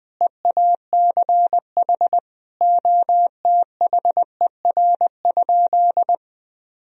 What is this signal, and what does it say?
Morse code, heard as a keyed tone, about 21 wpm, 700 Hz. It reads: EACH OTHER?
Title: Sentences from Top 200 Words and Abbreviations 20wpm